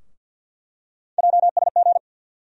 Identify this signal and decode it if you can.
Morse code, heard as a keyed tone, about 48 wpm, 700 Hz. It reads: JSG
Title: USA Call Sign Postfixes - 3 Letters 50wpm